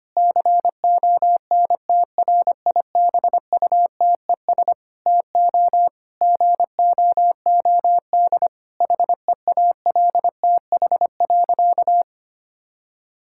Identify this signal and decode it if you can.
Morse code, heard as a keyed tone, about 25 wpm, 700 Hz. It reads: CONTRIBUTES TO GOOD HEALTH.
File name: practice / 25wpm / Difficult Sentences with Hints - 25wpm